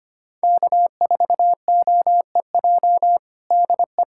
Morse code, heard as a keyed tone, about 25 wpm, 700 Hz. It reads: K4OEJ DE
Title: USA Callsign DE Callsign - 5 Characters - 25wpm